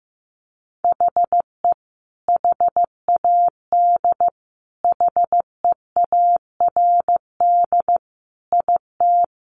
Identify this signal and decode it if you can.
Morse code, heard as a keyed tone, about 15 wpm, 700 Hz. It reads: HE HAD HEARD IT
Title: Sentences from Top 1400 Words 15wpm